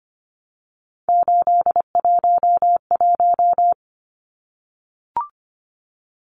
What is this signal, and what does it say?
Morse code, heard as a keyed tone, about 25 wpm, 700 Hz. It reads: 811
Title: Numbers - 3-Digit 25wpm